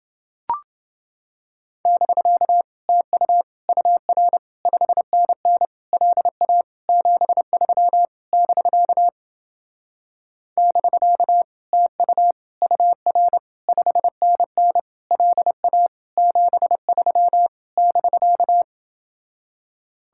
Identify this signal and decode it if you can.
Morse code, heard as a keyed tone, about 30 wpm, 700 Hz. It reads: <BK> TU UR 5NN LA 73 <BK> <BK> TU UR 5NN LA 73 <BK>